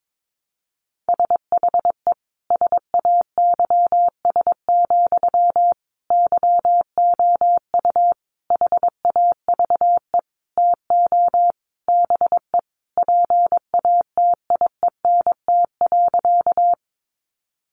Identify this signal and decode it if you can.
Morse code, heard as a keyed tone, about 22 wpm, 700 Hz. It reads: SHE SAYS, YOU HAVE TO BE PATIENT.
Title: Easy Phrases 22wpm